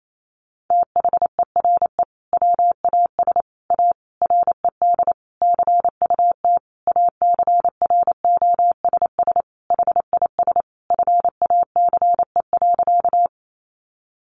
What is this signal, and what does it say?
Morse code, heard as a keyed tone, about 28 wpm, 700 Hz. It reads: THERE WAS A RED CUT ACROSS HIS FACE.
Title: Sentences from Top 700 Words 28wpm